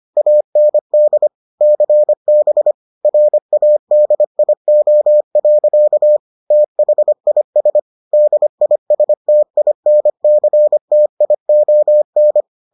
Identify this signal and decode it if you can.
Morse code, heard as a keyed tone, about 25 wpm, 600 Hz. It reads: AND CB RADIO. THIS DISTINCTION